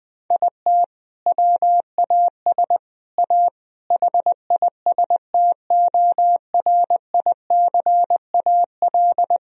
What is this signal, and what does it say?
Morse code, heard as a keyed tone, about 20 wpm, 700 Hz. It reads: IT WAS A HISTORICAL
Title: Sentences from Top 1000 Words - Review Part 4 - 20wpm